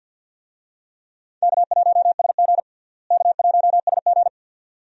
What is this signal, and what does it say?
Morse code, heard as a keyed tone, about 47 wpm, 700 Hz. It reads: K1SG K1SG
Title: USA Call Signs - 4 Characters - Repeated 2x - 50wpm